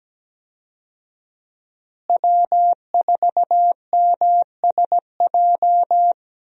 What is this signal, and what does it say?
Morse code, heard as a keyed tone, about 17 wpm, 700 Hz. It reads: W4MSJ
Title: USA Call Signs - 5 Characters - Repeated 2x - 17wpm